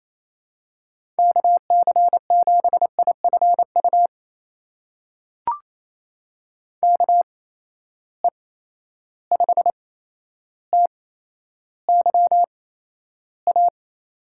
Morse code, heard as a keyed tone, about 28 wpm, spaced as for 12 wpm, 700 Hz. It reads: KC7IFU KE5TYA
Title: USA Call Signs - 6 Characters - 8x Character Spacing - 28wpm